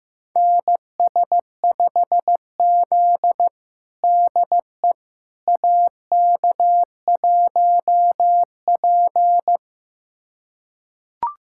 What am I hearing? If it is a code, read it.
Morse code, heard as a keyed tone, about 16 wpm, 700 Hz. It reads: NS5Z DE AK1P
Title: USA Callsign DE Callsign - 4 Characters - 15wpm